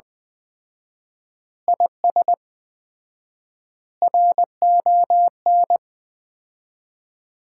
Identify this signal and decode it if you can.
Morse code, heard as a keyed tone, about 20 wpm, 700 Hz. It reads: IS RON
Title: Common QSO Phrases - 4x Word-Spacing - 20wpm